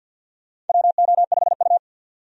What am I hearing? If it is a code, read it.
Morse code, heard as a keyed tone, about 53 wpm, 700 Hz. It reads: WO5U
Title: USA Call Signs - 4 Characters - 50wpm